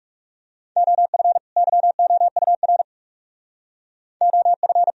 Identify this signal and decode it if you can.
Morse code, heard as a keyed tone, about 36 wpm, 700 Hz. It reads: OF YOUR OF
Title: Speed-Racing - Sets of 2 Words 30wpm